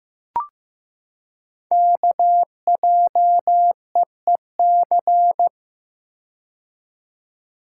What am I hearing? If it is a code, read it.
Morse code, heard as a keyed tone, about 15 wpm, 700 Hz. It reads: KJEEC
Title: Code Groups - 5 Letters - 15wpm